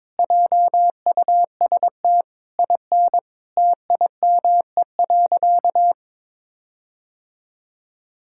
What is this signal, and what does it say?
Morse code, heard as a keyed tone, about 22 wpm, 700 Hz. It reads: JUST IN TIME.
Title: Sentences from Top 1600 Words 22wpm